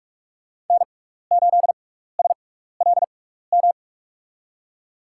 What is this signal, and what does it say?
Morse code, heard as a keyed tone, about 43 wpm, spaced as for 15 wpm, 700 Hz. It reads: N8SLM
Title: USA Call Signs - 5 Characters - 6x Character Spacing - 45wpm